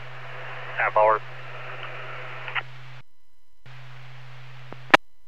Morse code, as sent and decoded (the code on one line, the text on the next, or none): none